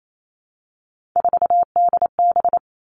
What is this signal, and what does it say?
Morse code, heard as a keyed tone, about 28 wpm, 700 Hz. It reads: 4DB
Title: All Cut Numbers - 3-Digit 28wpm